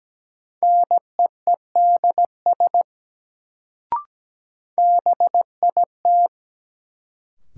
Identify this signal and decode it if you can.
Morse code, heard as a keyed tone, about 17 wpm, 700 Hz. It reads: NEEDS BIT
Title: Sets of 3 Words - New Word Drill - 17wpm